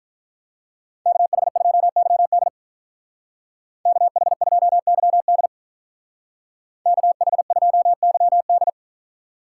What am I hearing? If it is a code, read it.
Morse code, heard as a keyed tone, about 45 wpm, 700 Hz. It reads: KH2YD KH2YD KH2YD